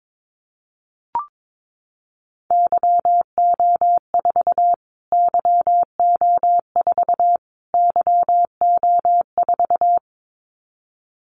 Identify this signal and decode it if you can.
Morse code, heard as a keyed tone, about 22 wpm, 700 Hz. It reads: YO4 YO4 YO4
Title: Callsign Prefixes for Top 25 DXCC - Repeated 3x - 22wpm